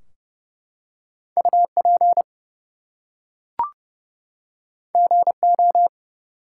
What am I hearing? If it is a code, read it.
Morse code, heard as a keyed tone, about 30 wpm, 700 Hz. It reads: UP GO
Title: Two Letter Words - 30wpm